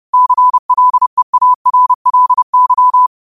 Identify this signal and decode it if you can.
Morse code, heard as a keyed tone, about 30 wpm, 1000 Hz. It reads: CLEARLY
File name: qotdfiles / 30wpm